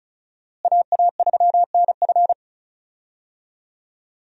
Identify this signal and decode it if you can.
Morse code, heard as a keyed tone, about 36 wpm, 700 Hz. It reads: AA3NF
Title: USA Call Signs - 5 Characters - Repeated 3x - 35wpm